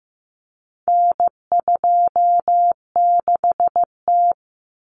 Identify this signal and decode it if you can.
Morse code, heard as a keyed tone, about 15 wpm, 700 Hz. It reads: N26T
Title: Common Cut Numbers - 4-Digit 15wpm